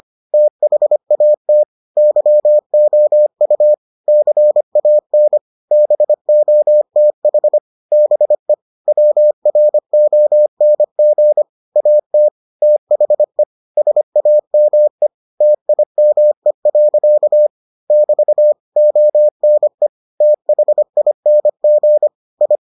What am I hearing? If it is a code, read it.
Morse code, heard as a keyed tone, about 25 wpm, 600 Hz. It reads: THAT YOU CAN BOTH BE WRONG AT THE SAME TIME. <BT> ONE THING I